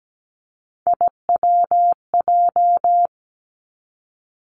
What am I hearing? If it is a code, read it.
Morse code, heard as a keyed tone, about 17 wpm, 700 Hz. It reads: IWJ
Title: Three Challenging Letter-Number Combinations 17wpm